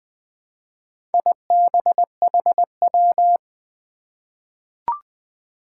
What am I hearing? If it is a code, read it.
Morse code, heard as a keyed tone, about 20 wpm, 700 Hz. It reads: IBHW
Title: Four Challenging Letter-Number Combinations 20wpm